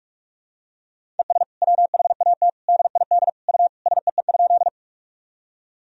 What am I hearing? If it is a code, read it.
Morse code, heard as a keyed tone, about 46 wpm, 700 Hz. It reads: ES WHAT DID U SEE?
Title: Sentences from Top 100 Words and Abbreviations 45wpm